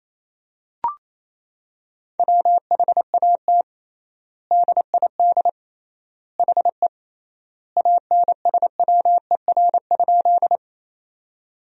Morse code, heard as a keyed tone, about 27 wpm, 700 Hz. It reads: WHAT DID HE ANSWER?
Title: 3x Word-Spacing - Sentences from Top 300 Words - 28wpm